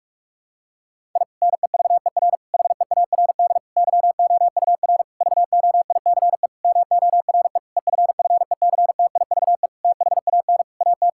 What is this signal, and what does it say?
Morse code, heard as a keyed tone, about 41 wpm, 700 Hz. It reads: I NEVER HEARD YOUR VOICE MORE EFFECTIVE THAN AT
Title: Sentences from Top 1000 Words - Review Part 6 - 45wpm